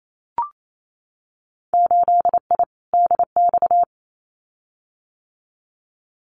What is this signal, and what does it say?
Morse code, heard as a keyed tone, about 28 wpm, 700 Hz. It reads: 8I DX
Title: Winter Field Day Exchanges Encore 28wpm